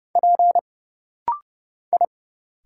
Morse code, heard as a keyed tone, about 31 wpm, 700 Hz. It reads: P I
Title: Single Letter-Number Alt - Rapid-Fire 30wpm